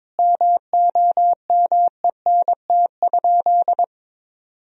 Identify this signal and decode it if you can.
Morse code, heard as a keyed tone, about 22 wpm, 700 Hz. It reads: MOMENT?
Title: Sentences from Top 1000 Words - Review Part 1 - 22wpm